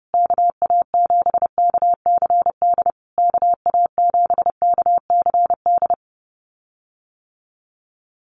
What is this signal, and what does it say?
Morse code, heard as a keyed tone, about 30 wpm, 700 Hz. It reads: KA7KCD KA7KCD
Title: USA Call Signs Repeated 3x 30wpm